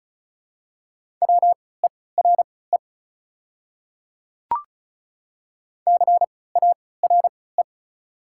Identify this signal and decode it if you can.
Morse code, heard as a keyed tone, about 34 wpm, spaced as for 20 wpm, 700 Hz. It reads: WERE CARE
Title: 3x Character Spacing - Top 300 Words - 35wpm